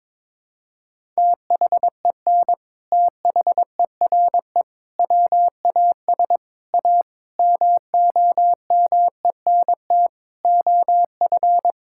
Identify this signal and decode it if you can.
Morse code, heard as a keyed tone, about 22 wpm, 700 Hz. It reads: THEN THERE WAS A MOMENT OF
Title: Sentences from Top 1600 Words 22wpm